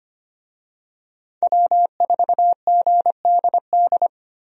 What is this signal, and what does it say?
Morse code, heard as a keyed tone, about 25 wpm, 700 Hz. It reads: W4GDD
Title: USA Call Signs - 5 Characters - 25wpm